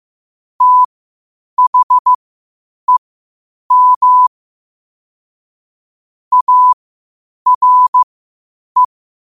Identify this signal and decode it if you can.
Morse code, heard as a keyed tone, about 15 wpm, spaced as for 8 wpm, 1000 Hz. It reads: THEM ARE
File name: qotdfiles / 8wpm